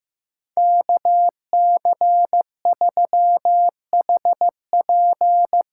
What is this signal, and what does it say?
Morse code, heard as a keyed tone, about 15 wpm, 700 Hz. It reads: KC3HP
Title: USA Call Signs - 6 Characters - 6x Character Spacing - 15wpm